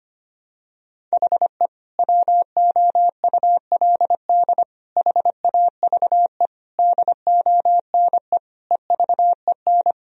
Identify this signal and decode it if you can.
Morse code, heard as a keyed tone, about 25 wpm, 700 Hz. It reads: HE WOULD HAVE DONE EVEN
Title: Sentences from Top 1000 Words - Review Part 1 - 25wpm